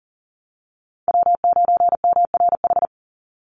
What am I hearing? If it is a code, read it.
Morse code, heard as a keyed tone, about 40 wpm, 700 Hz. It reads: W9MRH
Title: NA SOTA Chaser Callsigns 40wpm